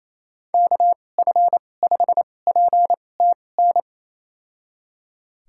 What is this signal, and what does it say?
Morse code, heard as a keyed tone, about 28 wpm, spaced as for 20 wpm, 700 Hz. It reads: KF5PTN
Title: USA Call Signs - 6 Characters - 2x Character Spacing - 28wpm